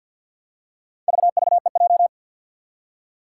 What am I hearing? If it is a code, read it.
Morse code, heard as a keyed tone, about 53 wpm, 700 Hz. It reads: VVEJ
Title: Four Challenging Letter-Number Combinations 50wpm